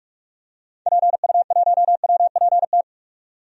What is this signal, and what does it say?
Morse code, heard as a keyed tone, about 43 wpm, 700 Hz. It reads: PU1WPT
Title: DXCC-100 Call Signs - 6 Characters - 45wpm